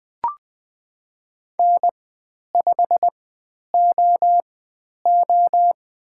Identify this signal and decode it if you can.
Morse code, heard as a keyed tone, about 20 wpm, spaced as for 10 wpm, 700 Hz. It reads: N5OO
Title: 095 Lesson 16g - Letter c - Call Signs 20wpm